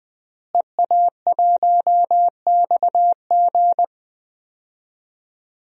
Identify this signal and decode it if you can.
Morse code, heard as a keyed tone, about 20 wpm, 700 Hz. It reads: EA1XG
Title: DXCC-100 Call Signs - 5 Characters - Repeated 2x - 20wpm